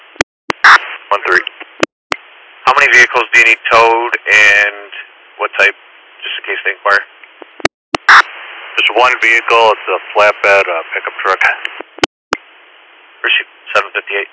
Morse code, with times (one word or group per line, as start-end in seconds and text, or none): none